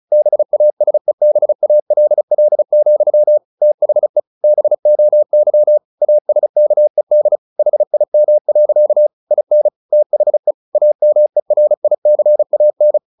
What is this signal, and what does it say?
Morse code, heard as a keyed tone, about 34 wpm, 600 Hz. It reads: BASEBALL, THE BOY ASKED HIM. IN THE AMERICAN
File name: the old man and the sea / the old man and the sea 1 35wpm